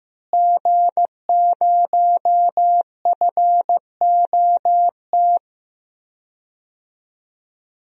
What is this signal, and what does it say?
Morse code, heard as a keyed tone, about 15 wpm, 700 Hz. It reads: G0FOT
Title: DXCC-100 Call Signs - 5 Characters - Repeated 3x - 15wpm